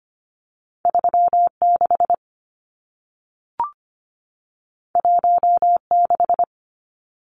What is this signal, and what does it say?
Morse code, heard as a keyed tone, about 25 wpm, 700 Hz. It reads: 3 6 1 6
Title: Numbers - 2-Digit 25wpm